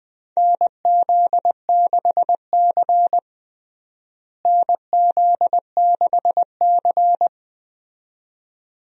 Morse code, heard as a keyed tone, about 21 wpm, 700 Hz. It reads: NZ6C NZ6C
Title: USA Call Signs - 4 Characters - Repeated 2x - 20wpm